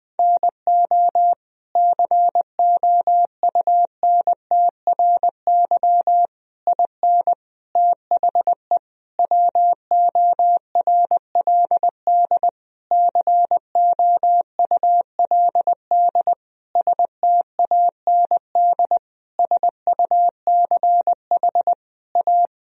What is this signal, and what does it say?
Morse code, heard as a keyed tone, about 21 wpm, 700 Hz. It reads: NO COUNTRY IN THE WORLD COULD STAND SUCH A
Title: Sentences from Top 1000 Words - Review Part 3 - 20wpm